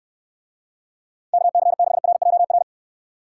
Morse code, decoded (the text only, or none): KC6RJR